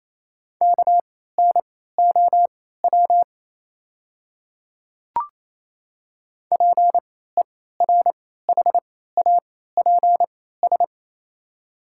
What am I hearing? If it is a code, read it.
Morse code, heard as a keyed tone, about 28 wpm, spaced as for 16 wpm, 700 Hz. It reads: KNOW PERHAPS
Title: 3x Character Spacing - Top 500 Words - 28wpm